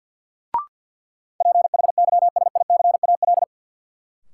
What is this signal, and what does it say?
Morse code, heard as a keyed tone, about 53 wpm, 700 Hz. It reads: PHYSICAL
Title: Sentences from Top 1400 Words - New Word Drill Alt - 50wpm